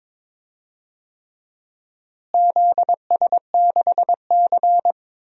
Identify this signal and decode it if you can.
Morse code, heard as a keyed tone, about 21 wpm, 700 Hz. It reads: ZS6C
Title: DXCC-100 Call Signs - 4 Characters - Repeated 3x - 22wpm